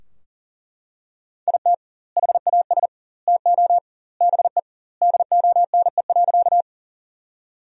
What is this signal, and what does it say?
Morse code, heard as a keyed tone, about 40 wpm, 700 Hz. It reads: IT HAS TO BE DONE.
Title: 2x Word-Spacing - Sentences from Top 500 Words - 40wpm